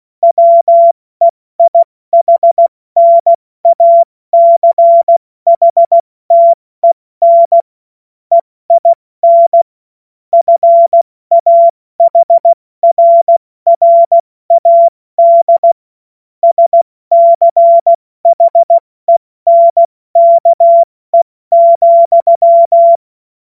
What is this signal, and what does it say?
Morse code, heard as a keyed tone, about 16 wpm, 680 Hz. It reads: WEIHNACHTEN EIN FAHRRAD SCHENKE,
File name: Morsekurs / Witze / Witz 14. Februar 2023 70 F80